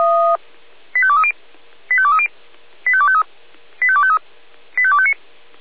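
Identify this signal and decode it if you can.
Five-tone selective call (ZVEI), heard as quick runs of five tones, tones of about 70 ms. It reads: 85230 85230 85253 85253 85259
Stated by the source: ZVEI five-tone selective call, received on an SDR.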